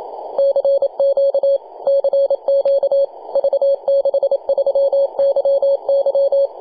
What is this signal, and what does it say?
Morse code, heard as a keyed tone, about 28 wpm, 550 Hz. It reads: CQ CQ V63YY
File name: posts / 2015-nov-21-0614z-v63yy-24-905mhz-cw